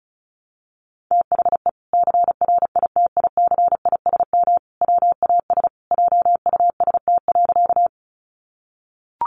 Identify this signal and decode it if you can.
Morse code, heard as a keyed tone, about 34 wpm, 700 Hz. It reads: THE CRITICISM WAS JUST.
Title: Sentences from Top 1300 Words 35wpm